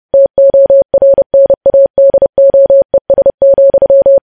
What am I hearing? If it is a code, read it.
Morse code, heard as a keyed tone, about 30 wpm, 550 Hz. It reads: TORNADOES,
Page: News Headlines in Morse Code at 30 WPM